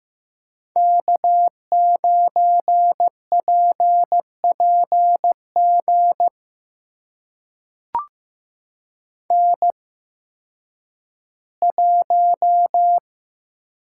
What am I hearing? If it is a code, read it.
Morse code, heard as a keyed tone, about 15 wpm, spaced as for 10 wpm, 700 Hz. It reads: K9PPG N1
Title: USA Call Signs - 5 Characters - 8x Character Spacing - 15wpm